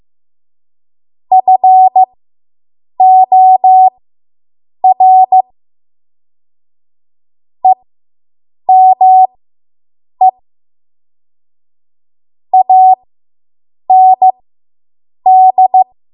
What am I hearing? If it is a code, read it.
Morse code, heard as a keyed tone, about 15 wpm, spaced as for 7 wpm, 750 Hz. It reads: FOR EME AND